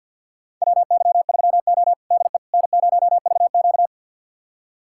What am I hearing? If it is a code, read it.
Morse code, heard as a keyed tone, about 50 wpm, 700 Hz. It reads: WY3O DE N0VQ
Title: USA Callsign DE Callsign - 4 Characters - 50wpm